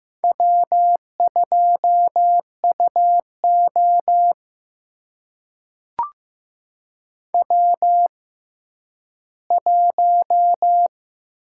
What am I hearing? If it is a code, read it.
Morse code, heard as a keyed tone, about 15 wpm, spaced as for 10 wpm, 700 Hz. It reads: W2UO W1
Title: USA Call Signs - 6x Character Spacing - 15wpm